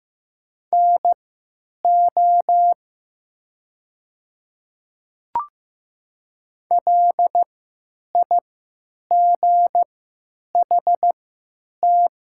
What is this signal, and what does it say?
Morse code, heard as a keyed tone, about 15 wpm, spaced as for 9 wpm, 700 Hz. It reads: NO LIGHT
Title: 3x Character Spacing - Top 200 Words - 15wpm